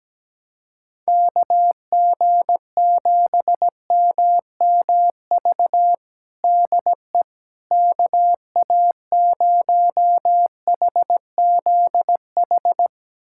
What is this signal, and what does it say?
Morse code, heard as a keyed tone, about 17 wpm, 700 Hz. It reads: KG7MMV DE KA0HZH
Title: USA Callsign DE Callsign 17wpm